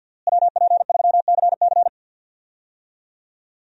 Morse code, heard as a keyed tone, about 49 wpm, 700 Hz. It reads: WP3CC
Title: USA Call Signs - 5 Characters - Repeated 2x - 50wpm